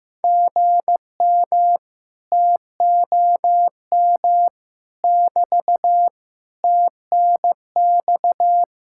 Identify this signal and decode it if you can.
Morse code, heard as a keyed tone, about 15 wpm, 700 Hz. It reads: GM TOM <BT> TNX
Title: Speed-Racing Common QSO Phrases Encore - 15wpm